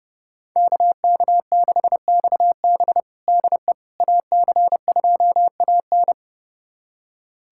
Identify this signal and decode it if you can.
Morse code, heard as a keyed tone, about 30 wpm, 700 Hz. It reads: KK6XB DE AC2AN
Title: USA Callsign DE Callsign - 5 Characters - 30wpm